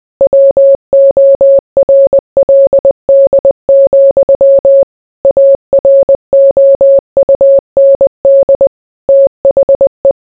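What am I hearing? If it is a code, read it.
Morse code, heard as a keyed tone, about 20 wpm, 550 Hz. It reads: WORLD, AROUND THE
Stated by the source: Morse code mixed with noise and music as background sound for sleep.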